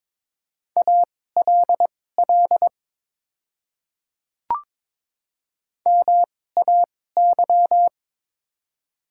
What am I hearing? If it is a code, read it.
Morse code, heard as a keyed tone, about 22 wpm, spaced as for 16 wpm, 700 Hz. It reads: ALL MAY
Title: 2x Character Spacing - Top 300 Words - 22wpm